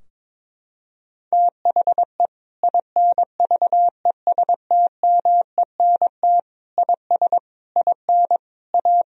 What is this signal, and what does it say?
Morse code, heard as a keyed tone, about 22 wpm, 700 Hz. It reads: THE INVESTMENT IS IN A
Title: Sentences from Top 1000 Words - Review Part 4 - 22wpm